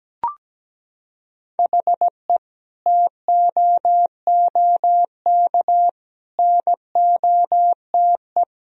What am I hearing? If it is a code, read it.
Morse code, heard as a keyed tone, about 17 wpm, 700 Hz. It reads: HE TOOK NOTE
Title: Sentences from Top 1000 Words - Review Part 2 - 17wpm